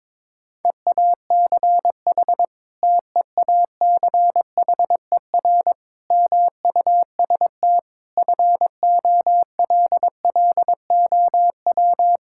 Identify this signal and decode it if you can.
Morse code, heard as a keyed tone, about 22 wpm, 700 Hz. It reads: EACH TEACHER MUST FOLLOW
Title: Sentences from Top 1000 Words - Review Part 6 - 22wpm